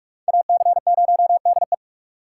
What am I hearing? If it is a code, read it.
Morse code, heard as a keyed tone, about 44 wpm, 700 Hz. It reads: AC0DE